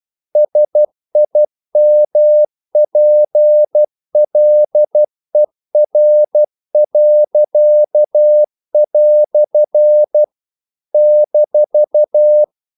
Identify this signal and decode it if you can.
Morse code, heard as a keyed tone, about 12 wpm, 600 Hz. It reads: SIMPLER." -
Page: English quotations in CW (telegraphy - morse)